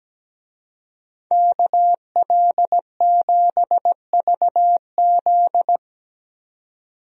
Morse code, heard as a keyed tone, about 17 wpm, 700 Hz. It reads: KL7VZ
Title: DXCC-100 Call Signs - 5 Characters - 17wpm